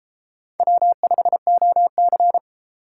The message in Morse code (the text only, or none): W5OC